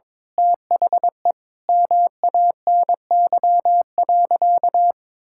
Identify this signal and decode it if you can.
Morse code, heard as a keyed tone, about 21 wpm, 700 Hz. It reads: THE MANY.
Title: Sentences from Top 1000 Words - Review Part 6 - 22wpm